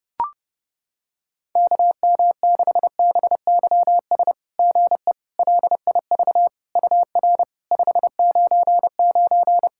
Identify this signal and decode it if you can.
Morse code, heard as a keyed tone, about 30 wpm, 700 Hz. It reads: KM6BYS GE LIV UR 599
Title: SOTA QSO Practice 30wpm